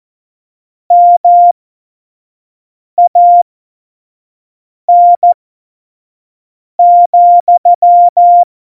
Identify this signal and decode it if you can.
Morse code, heard as a keyed tone, about 14 wpm, spaced as for 5 wpm, 700 Hz. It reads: MAN,
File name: Morsekurs / Witze / Witz 19 Mai 2020, 25 F70